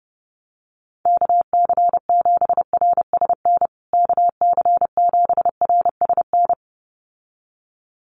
Morse code, heard as a keyed tone, about 30 wpm, 700 Hz. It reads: KC7RSN KC7RSN
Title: USA Call Signs Repeated 2x 30wpm